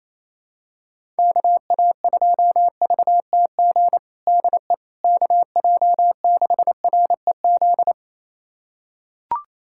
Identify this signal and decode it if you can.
Morse code, heard as a keyed tone, about 28 wpm, 700 Hz. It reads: KA2VTG DE KJ6REZ
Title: USA Callsign DE Callsign - 6 Characters - 28wpm